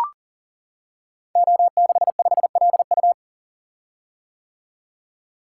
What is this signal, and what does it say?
Morse code, heard as a keyed tone, about 39 wpm, 700 Hz. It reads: O65LU
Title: Code Groups - 5 Letter:Numbers - 40wpm